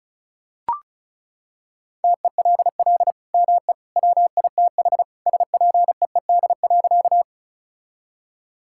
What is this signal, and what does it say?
Morse code, heard as a keyed tone, about 36 wpm, 700 Hz. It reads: TELL ME WITH SPEED.
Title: Sentences from Top 1000 Words - Review Part 5 - 35wpm